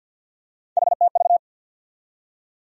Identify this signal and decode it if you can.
Morse code, heard as a keyed tone, about 53 wpm, 700 Hz. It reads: HTV
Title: USA Call Sign Postfixes - 3 Letters 50wpm